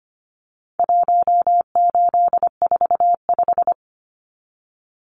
Morse code, heard as a keyed tone, about 25 wpm, 700 Hz. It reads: 1845
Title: Numbers - 4-Digit 25wpm